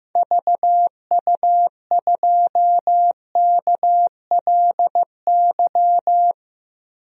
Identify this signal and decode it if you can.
Morse code, heard as a keyed tone, about 15 wpm, 700 Hz. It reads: VU2KLY
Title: DXCC-100 Call Signs - 6 Characters - 15wpm